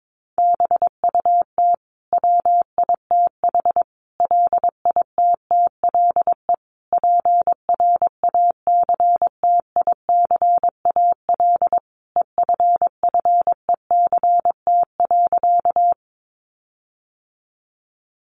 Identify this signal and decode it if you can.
Morse code, heard as a keyed tone, about 22 wpm, 700 Hz. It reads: BUT WITH LITTLE PRACTICAL EFFECT.
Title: Sentences from Top 1400 Words 22wpm